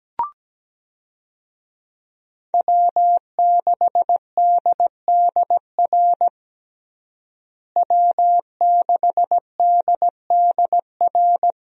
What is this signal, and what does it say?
Morse code, heard as a keyed tone, about 17 wpm, 700 Hz. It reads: W6DDR W6DDR
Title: USA Call Signs - 5 Characters - Repeated 3x - 17wpm